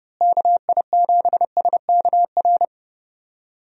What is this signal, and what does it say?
Morse code, heard as a keyed tone, about 30 wpm, 700 Hz. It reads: KI7SKR
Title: USA Call Signs - 6 Characters - Repeated 2x - 30wpm